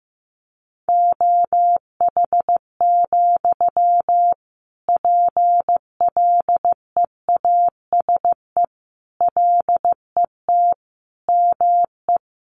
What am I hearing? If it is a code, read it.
Morse code, heard as a keyed tone, about 15 wpm, 700 Hz. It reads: OH, PLEASE LET ME
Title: Sentences from Top 1500 Words 15wpm